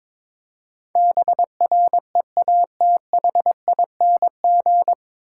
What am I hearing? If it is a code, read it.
Morse code, heard as a keyed tone, about 22 wpm, 700 Hz. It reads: BREATHING
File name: 9 Letter Words - First 3 Letters Given - 22wpm